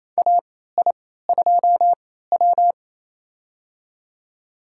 Morse code, heard as a keyed tone, about 27 wpm, spaced as for 16 wpm, 700 Hz. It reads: AI2W
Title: USA Call Signs - 4 Characters - 3x Character Spacing - 28wpm